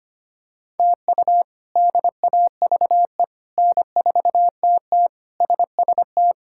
Morse code, heard as a keyed tone, about 25 wpm, 700 Hz. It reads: TU DAVE N4TT SST
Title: K1USN SST QSO Practice - 25wpm